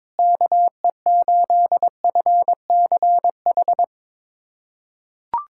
Morse code, heard as a keyed tone, about 22 wpm, 700 Hz. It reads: KE8FCH
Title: USA Callsign DE Callsign - 6 Characters - 22wpm